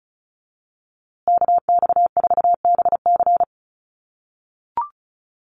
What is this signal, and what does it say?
Morse code, heard as a keyed tone, about 34 wpm, 700 Hz. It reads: KX4BC
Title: USA Call Signs Repeated 2x 35wpm